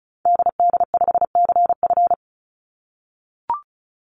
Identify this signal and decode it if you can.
Morse code, heard as a keyed tone, about 34 wpm, 700 Hz. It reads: DD5CF
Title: Call Signs - 35wpm